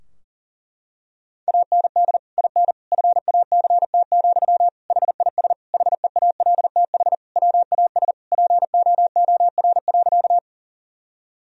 Morse code, heard as a keyed tone, about 37 wpm, 700 Hz. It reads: AND IN FACT, HIS HEALTH WAS POOR.
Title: Sentences from Top 1000 Words - Review Part 3 - 40wpm